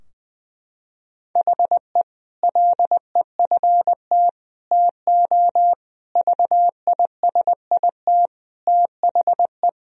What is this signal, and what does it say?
Morse code, heard as a keyed tone, about 20 wpm, 700 Hz. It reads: HE LEFT TO VISIT THE
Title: Sentences from Top 1000 Words - Review Part 2 - 20wpm